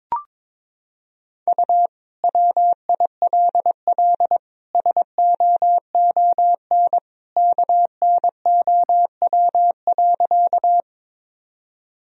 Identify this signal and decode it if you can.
Morse code, heard as a keyed tone, about 22 wpm, 700 Hz. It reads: U WILL SOON KNOW.
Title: Sentences from Top 400 Words and Abbreviations 22wpm